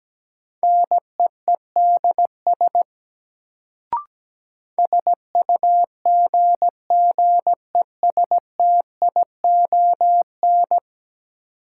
Morse code, heard as a keyed tone, about 17 wpm, 700 Hz. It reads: NEEDS SUGGESTION
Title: Sentences from Top 1600 Words - New Word Drill Alt - 17wpm